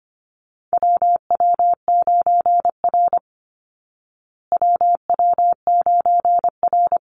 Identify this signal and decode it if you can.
Morse code, heard as a keyed tone, about 25 wpm, 700 Hz. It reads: WW9R WW9R
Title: Call Signs Repeated 3x - 25wpm